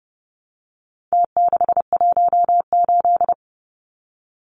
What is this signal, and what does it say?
Morse code, heard as a keyed tone, about 30 wpm, 700 Hz. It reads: T618
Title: Numbers - Contest Style - 4-Digit 30wpm